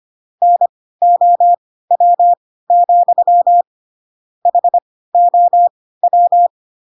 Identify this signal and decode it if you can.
Morse code, heard as a keyed tone, about 25 wpm, spaced as for 16 wpm, 700 Hz. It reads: NOW, HOW
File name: Pinocchio / Pinocchio ch5 10 25 ik1hge